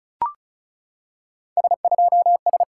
Morse code, heard as a keyed tone, about 34 wpm, 700 Hz. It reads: S2S
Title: SOTA QSO Practice 35wpm